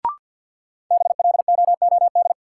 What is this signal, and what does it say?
Morse code, heard as a keyed tone, about 52 wpm, 700 Hz. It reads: BLOOD
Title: Sentences from Top 1100 Words - New Word Drill Alt - 50wpm